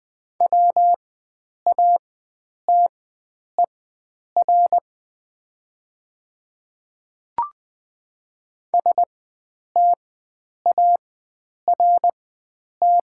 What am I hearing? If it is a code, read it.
Morse code, heard as a keyed tone, about 20 wpm, spaced as for 9 wpm, 700 Hz. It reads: WATER START
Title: 4x Character Spacing - Top 500 Words - 20wpm